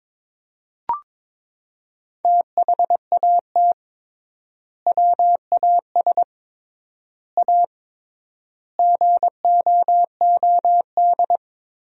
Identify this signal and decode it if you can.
Morse code, heard as a keyed tone, about 22 wpm, 700 Hz. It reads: THAT WAS A GOOD
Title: 3x Word-Spacing - Sentences from Top 200 Words - 22wpm